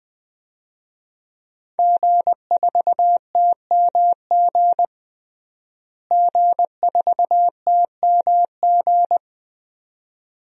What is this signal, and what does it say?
Morse code, heard as a keyed tone, about 20 wpm, 700 Hz. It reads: G4TMG G4TMG
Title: DXCC-100 Call Signs - 5 Characters - Repeated 2x - 20wpm